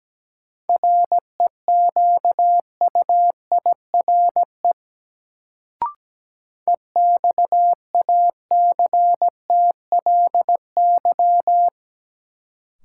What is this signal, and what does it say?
Morse code, heard as a keyed tone, about 17 wpm, 700 Hz. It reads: REQUIRE EXACTLY
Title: Sentences from Top 900 Words - New Word Drill - 17wpm